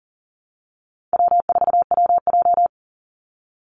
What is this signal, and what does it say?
Morse code, heard as a keyed tone, about 40 wpm, 700 Hz. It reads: W4WJ
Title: Easy Call Signs Repeated 2x - 40wpm